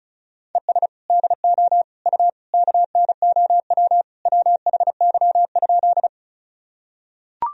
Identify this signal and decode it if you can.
Morse code, heard as a keyed tone, about 36 wpm, 700 Hz. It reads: ES DO U KNOW WHY?
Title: Sentences from Top 200 Words and Abbreviations 35wpm